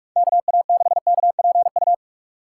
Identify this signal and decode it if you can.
Morse code, heard as a keyed tone, about 46 wpm, 700 Hz. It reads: KA6KPU
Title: USA Call Signs - 6 Characters - 3x Character Spacing - 45wpm